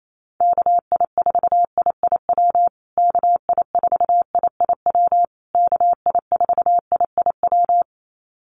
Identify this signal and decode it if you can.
Morse code, heard as a keyed tone, about 28 wpm, 700 Hz. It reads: KI4IIW KI4IIW KI4IIW
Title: USA Call Signs Repeated 3x 28wpm